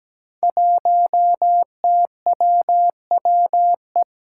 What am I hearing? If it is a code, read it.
Morse code, heard as a keyed tone, about 17 wpm, 700 Hz. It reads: 1TWWE
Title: Five Challenging Letter-Number Combinations 17wpm